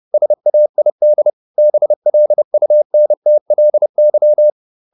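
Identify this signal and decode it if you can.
Morse code, heard as a keyed tone, about 30 wpm, 600 Hz. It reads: SAID BLUNTLY
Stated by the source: Morse code at 30 wpm, the novel Never by Ken Follett in English, from CW med Gnister.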